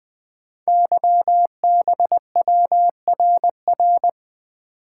Morse code, heard as a keyed tone, about 20 wpm, 700 Hz. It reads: YBWRR
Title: Code Groups - 5 Letters - 20wpm